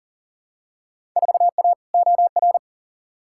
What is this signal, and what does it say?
Morse code, heard as a keyed tone, about 40 wpm, 700 Hz. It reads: 4A OR
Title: Field Day Exchanges Encore - 40wpm